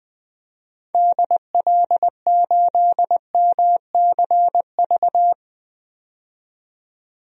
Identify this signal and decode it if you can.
Morse code, heard as a keyed tone, about 20 wpm, 700 Hz. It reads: DL8MCV
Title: DXCC-100 Call Signs - 6 Characters - 20wpm